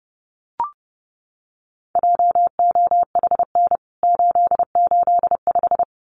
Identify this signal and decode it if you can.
Morse code, heard as a keyed tone, about 30 wpm, 700 Hz. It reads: JOHN 885
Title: CWT Exchange 30wpm